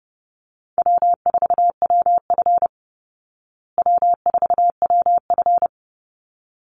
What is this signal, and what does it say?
Morse code, heard as a keyed tone, about 30 wpm, 700 Hz. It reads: W4WF W4WF
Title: Easy Call Signs Repeated 2x - 30wpm